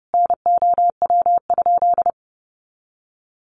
Morse code, heard as a keyed tone, about 30 wpm, 700 Hz. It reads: NOW?
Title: Common QSO Phrases 30wpm